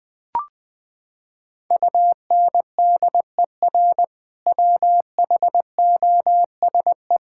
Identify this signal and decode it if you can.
Morse code, heard as a keyed tone, about 20 wpm, 700 Hz. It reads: UNDER WHOSE
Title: Sentences from Top 1600 Words 20wpm